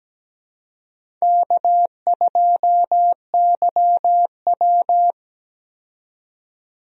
Morse code, heard as a keyed tone, about 17 wpm, 700 Hz. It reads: K2YW
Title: USA Call Signs - 4 Characters - 17wpm